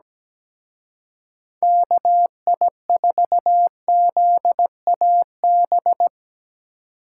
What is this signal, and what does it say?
Morse code, heard as a keyed tone, about 19 wpm, 700 Hz. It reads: KI4ZAB
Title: Speed Racing - USA 2x3 Call Signs 17wpm